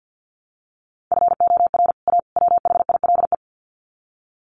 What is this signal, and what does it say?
Morse code, heard as a keyed tone, about 48 wpm, 700 Hz. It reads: FOR A WHILE
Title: Sets of 3 Words 50wpm